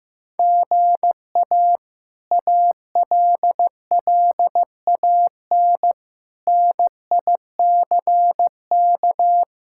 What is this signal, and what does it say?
Morse code, heard as a keyed tone, about 16 wpm, 700 Hz. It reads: GA ALLAN NICK
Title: K1USN SST QSO Practice - 15wpm